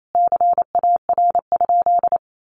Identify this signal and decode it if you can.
Morse code, heard as a keyed tone, about 28 wpm, 700 Hz. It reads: CAR?
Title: Sentences from Top 1500 Words 28wpm